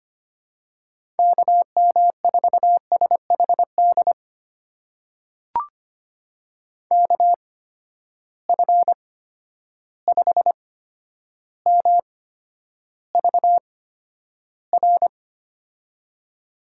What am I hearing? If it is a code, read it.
Morse code, heard as a keyed tone, about 25 wpm, spaced as for 11 wpm, 700 Hz. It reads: KM4SHD KF5MVR
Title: USA Call Signs - 6 Characters - 8x Character Spacing - 25wpm